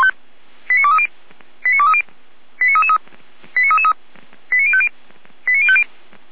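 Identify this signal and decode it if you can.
Five-tone selective call (ZVEI), heard as quick runs of five tones, tones of about 70 ms. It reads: 89230 89230 89333 89050 89050
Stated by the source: ZVEI five-tone call received by SDR.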